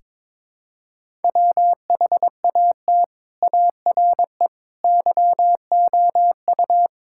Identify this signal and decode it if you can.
Morse code, heard as a keyed tone, about 22 wpm, 700 Hz. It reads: WHAT ARE YOU
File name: practice / 22wpm / Build A Phrase - 5-Word Phrases - 22wpm